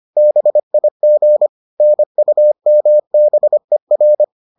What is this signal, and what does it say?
Morse code, heard as a keyed tone, about 25 wpm, 600 Hz. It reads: BIG NUMBER